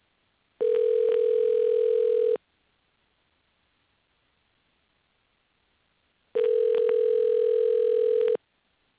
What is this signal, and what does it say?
Ringback tone (North America), 440 + 480 Hz.